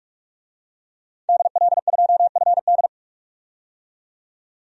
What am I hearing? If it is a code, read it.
Morse code, heard as a keyed tone, about 42 wpm, 700 Hz. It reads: DL2FD
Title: DXCC-100 Call Signs - 5 Characters - 45wpm